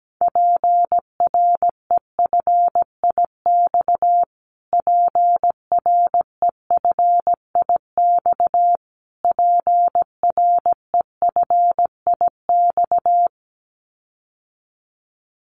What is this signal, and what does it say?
Morse code, heard as a keyed tone, about 17 wpm, 700 Hz. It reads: PREFIX PREFIX PREFIX
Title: Callsign Prefixes for Top 25 DXCC - Repeated 3x - 17wpm